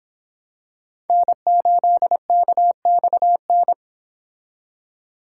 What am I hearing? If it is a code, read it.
Morse code, heard as a keyed tone, about 26 wpm, 700 Hz. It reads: N8KXN